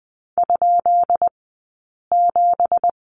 Morse code, heard as a keyed tone, about 20 wpm, spaced as for 8 wpm, 700 Hz. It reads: ?7
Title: 217 Lesson 35d - Number 0 - Two Characters 20wpm